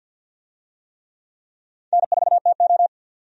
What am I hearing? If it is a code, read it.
Morse code, heard as a keyed tone, about 52 wpm, 700 Hz. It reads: N4TO
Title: USA Call Signs - 4 Characters - 50wpm